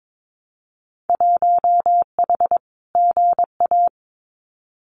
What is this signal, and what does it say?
Morse code, heard as a keyed tone, about 22 wpm, 700 Hz. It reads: 1H GA
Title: Winter Field Day Exchanges 22wpm